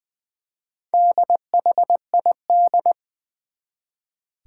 Morse code, heard as a keyed tone, about 20 wpm, 700 Hz. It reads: DHID
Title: Four Challenging Letter-Number Combinations 20wpm